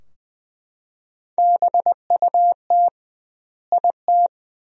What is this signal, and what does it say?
Morse code, heard as a keyed tone, about 20 wpm, 700 Hz. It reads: BUT IT
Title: 2x Word-Spacing - Sentences from Top 300 Words - 20wpm